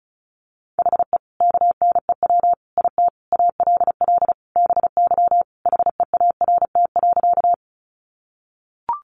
Morse code, heard as a keyed tone, about 34 wpm, 700 Hz. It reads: HE KNEW IT ALL BY HEART.
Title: Sentences from Top 1200 Words 35wpm